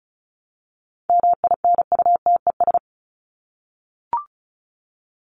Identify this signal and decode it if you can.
Morse code, heard as a keyed tone, about 34 wpm, 700 Hz. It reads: MINUTES
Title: ES:S Suffix - Top 100 Words with Base Words 35wpm